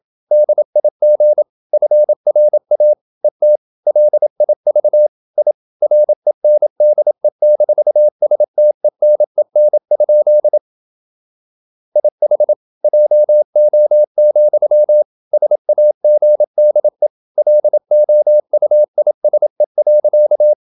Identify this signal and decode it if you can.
Morse code, heard as a keyed tone, about 27 wpm, 600 Hz. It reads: DIG FRA ET LIV I RENDE-STENEN? IH JO, SAGDE LOUISE.
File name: Ferie QRQ 27wpm